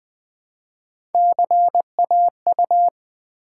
Morse code, heard as a keyed tone, about 20 wpm, 700 Hz. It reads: CAU
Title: Build A Word - 1 Letter At A Time - 5-Letter Words - 20wpm